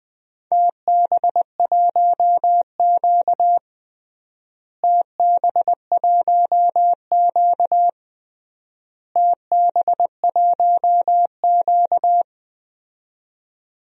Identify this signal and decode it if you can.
Morse code, heard as a keyed tone, about 21 wpm, 700 Hz. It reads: TB1Q TB1Q TB1Q
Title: DXCC-100 Call Signs - 4 Characters - Repeated 3x - 20wpm